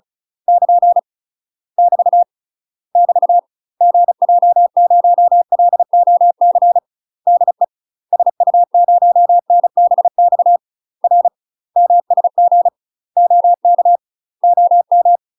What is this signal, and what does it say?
Morse code, heard as a keyed tone, about 36 wpm, 700 Hz. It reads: <KN> <BT> <BT> GJ0LOC DE SU0NBX R MSG OK OM
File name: UK MASTER QSOs TRAINING / 35WPM / UK MASTER QSOs 0012